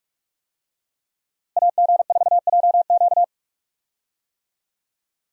Morse code, heard as a keyed tone, about 42 wpm, 700 Hz. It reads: AG4JQ
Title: USA Call Signs - 5 Characters - Repeated 2x - 45wpm